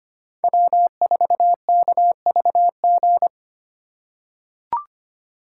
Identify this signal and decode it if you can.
Morse code, heard as a keyed tone, about 25 wpm, 700 Hz. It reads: W4KVG
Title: USA Call Signs - 2x Character Spacing - 25wpm